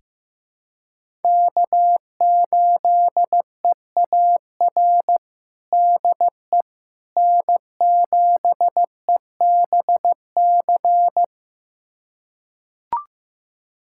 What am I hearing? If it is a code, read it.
Morse code, heard as a keyed tone, about 15 wpm, 700 Hz. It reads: K8EAR DE N7EBC
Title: USA Callsign DE Callsign - 5 Characters - 15wpm